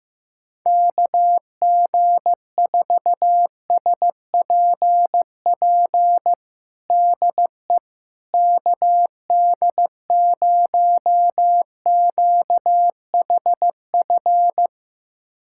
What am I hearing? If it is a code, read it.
Morse code, heard as a keyed tone, about 15 wpm, 700 Hz. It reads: KG4SPP DE KD0QHF
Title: USA Callsign DE Callsign - 6 Characters - 15wpm